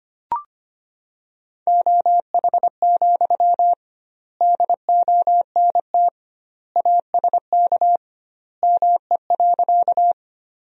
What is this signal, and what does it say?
Morse code, heard as a keyed tone, about 25 wpm, 700 Hz. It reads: OH, DONT ASK ME.
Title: 2x Word-Spacing - Sentences from Top 500 Words - 25wpm